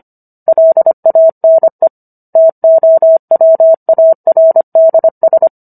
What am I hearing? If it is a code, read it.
Morse code, heard as a keyed tone, about 25 wpm, 650 Hz. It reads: LANE TOWARDS